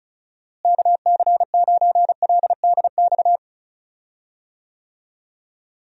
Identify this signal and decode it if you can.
Morse code, heard as a keyed tone, about 36 wpm, 700 Hz. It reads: KC9LDX
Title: USA Call Signs - 6 Characters - Repeated 3x - 35wpm